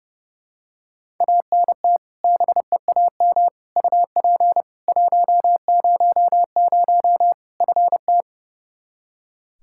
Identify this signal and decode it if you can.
Morse code, heard as a keyed tone, about 30 wpm, 700 Hz. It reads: ANT BEAM UP 100 FT
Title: K1USN SST QSO Practice 30wpm